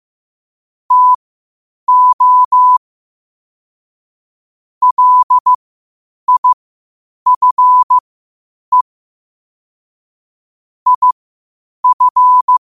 Morse code, heard as a keyed tone, about 15 wpm, spaced as for 8 wpm, 1000 Hz. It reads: TO LIFE IF
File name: qotdfiles / 8wpm